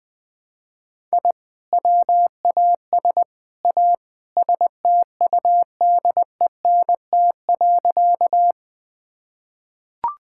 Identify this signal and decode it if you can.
Morse code, heard as a keyed tone, about 20 wpm, 700 Hz. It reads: I WAS A STUDENT.
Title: Sentences with 3-Word Phrases - Spoken Beginning - 20wpm